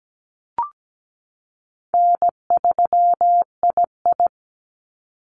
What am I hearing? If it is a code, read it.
Morse code, heard as a keyed tone, about 17 wpm, 700 Hz. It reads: N3II
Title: NA SOTA Activator Callsigns 17wpm